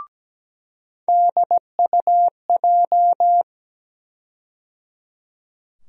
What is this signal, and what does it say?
Morse code, heard as a keyed tone, about 17 wpm, 700 Hz. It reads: DUJ
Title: USA Call Sign Postfixes - 3 Letters 17wpm